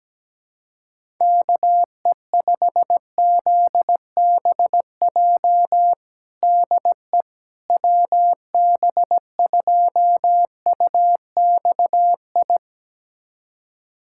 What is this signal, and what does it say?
Morse code, heard as a keyed tone, about 17 wpm, 700 Hz. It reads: KE5ZBJ DE WB2UXI
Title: USA Callsign DE Callsign 17wpm